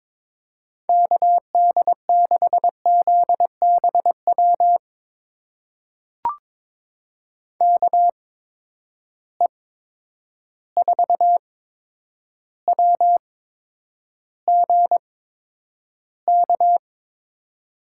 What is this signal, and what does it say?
Morse code, heard as a keyed tone, about 22 wpm, spaced as for 9 wpm, 700 Hz. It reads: KD6ZBW KE4WGK
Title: USA Call Signs - 6 Characters - 8x Character Spacing - 22wpm